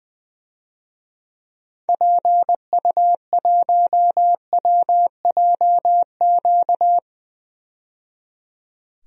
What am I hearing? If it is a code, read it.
Morse code, heard as a keyed tone, about 20 wpm, 700 Hz. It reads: PU1WJQ
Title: DXCC-100 Call Signs - 6 Characters - 20wpm